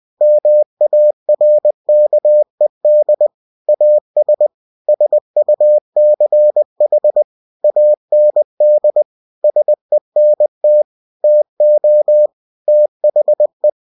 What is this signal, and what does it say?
Morse code, heard as a keyed tone, about 20 wpm, 600 Hz. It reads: MARKED AS SUCH AND SENT TO THE
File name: cw-books / winnie-the-pooh / winnie-20wpm-page-0117